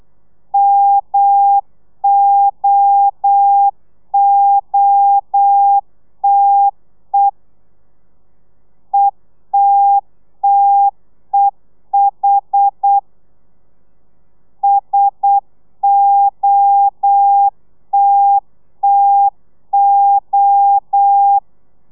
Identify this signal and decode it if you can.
Morse code, heard as a keyed tone, about 8 wpm, 800 Hz. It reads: MOOTE ETTEH SOTTO
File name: telegrafia / i2tfj / Lez1 35